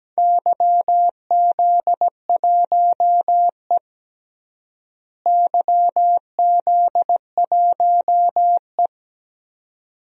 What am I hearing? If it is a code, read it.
Morse code, heard as a keyed tone, about 17 wpm, 700 Hz. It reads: YZ1E YZ1E
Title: DXCC-100 Call Signs - 4 Characters - Repeated 2x - 17wpm